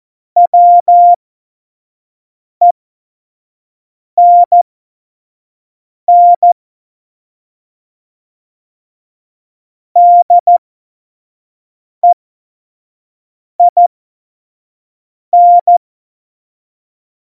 Morse code, heard as a keyed tone, about 14 wpm, spaced as for 5 wpm, 700 Hz. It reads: WENN DEIN
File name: Morsekurs / Witze / Witz 05. September 17, Tempo 25